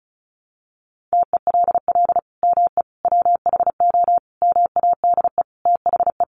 Morse code, heard as a keyed tone, about 34 wpm, 700 Hz. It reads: TELL ME WHO MADE THE
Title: Sentences from Top 200 Words 35wpm